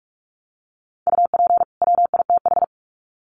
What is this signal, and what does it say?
Morse code, heard as a keyed tone, about 44 wpm, 700 Hz. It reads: UP WITH
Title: Sets of 2 Words 45wpm